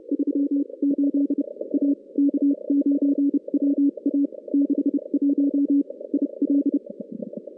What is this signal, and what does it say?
Morse code, heard as a keyed tone, about 30 wpm, spaced as for 26 wpm, 280 Hz. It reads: 38 A K9WA 61 IL
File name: contests / CW-SS-k5tr- -2100